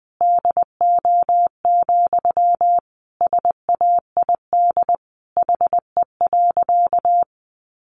Morse code, heard as a keyed tone, about 20 wpm, 700 Hz. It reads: DO, SAID HE.
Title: Sentences from Top 100 Words 20wpm